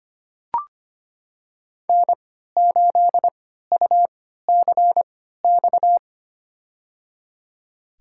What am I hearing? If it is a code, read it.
Morse code, heard as a keyed tone, about 25 wpm, spaced as for 14 wpm, 700 Hz. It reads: N8UCX
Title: USA Call Signs - 5 Characters - 3x Character Spacing - 25wpm